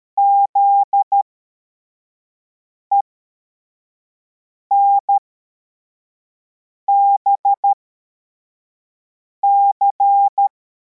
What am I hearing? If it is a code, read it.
Morse code, heard as a keyed tone, about 13 wpm, spaced as for 4 wpm, 800 Hz. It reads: ZENBC
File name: formation / cw / Text85